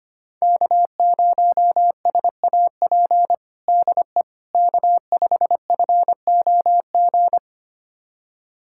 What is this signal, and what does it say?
Morse code, heard as a keyed tone, about 25 wpm, 700 Hz. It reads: K0SAP DE K5FOG
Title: USA Callsign DE Callsign - 5 Characters - 25wpm